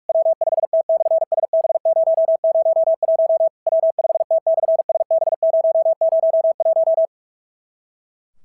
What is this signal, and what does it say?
Morse code, heard as a keyed tone, about 44 wpm, 650 Hz. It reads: W5T/SB001 W5T/SB001
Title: SOTA QSO Practice 45wpm